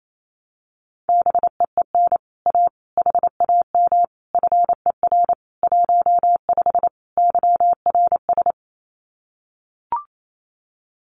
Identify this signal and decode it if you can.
Morse code, heard as a keyed tone, about 28 wpm, 700 Hz. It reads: BEEN A HAM FER 15 YRS
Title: Common QSO Phrases Encore 28wpm